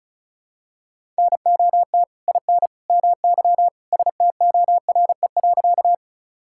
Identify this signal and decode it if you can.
Morse code, heard as a keyed tone, about 36 wpm, 700 Hz. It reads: NOT IN MY STORE.
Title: Sentences from Top 1000 Words - Review Part 5 - 35wpm